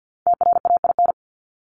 Morse code, heard as a keyed tone, about 49 wpm, 700 Hz. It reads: TRAIN
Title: ING Suffix - Top 100 Words with Base Words 50wpm